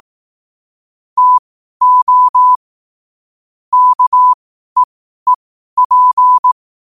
Morse code, heard as a keyed tone, about 18 wpm, spaced as for 12 wpm, 1000 Hz. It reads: TO KEEP
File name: qotdfiles / 12wpm